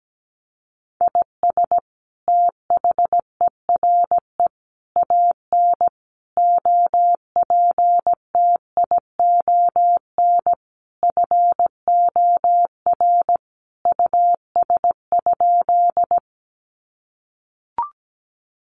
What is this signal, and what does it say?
Morse code, heard as a keyed tone, about 17 wpm, 700 Hz. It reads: IS THERE AN OPTION FOR US?
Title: Sentences from Top 1300 Words 17wpm